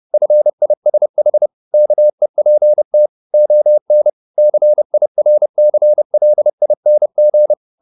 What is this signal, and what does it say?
Morse code, heard as a keyed tone, about 30 wpm, 600 Hz. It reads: FISH KEPT ON CIRCLING